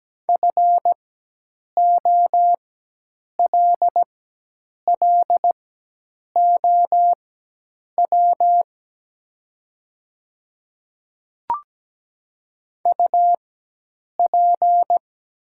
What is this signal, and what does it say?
Morse code, heard as a keyed tone, about 17 wpm, spaced as for 8 wpm, 700 Hz. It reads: FOLLOW UP
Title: 4x Character Spacing - Top 200 Words - 17wpm